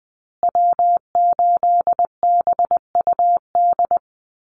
Morse code, heard as a keyed tone, about 20 wpm, 700 Hz. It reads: W8BUD
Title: Call Signs Repeated 2x - 20wpm